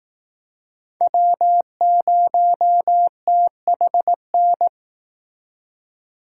Morse code, heard as a keyed tone, about 18 wpm, 700 Hz. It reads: W0THN
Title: Speed Racing - USA 1x3 Call Signs 15wpm